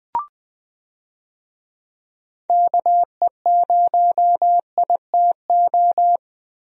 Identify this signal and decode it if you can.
Morse code, heard as a keyed tone, about 20 wpm, 700 Hz. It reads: KE0ITO
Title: USA Call Signs - 6 Characters - Repeated 2x - 20wpm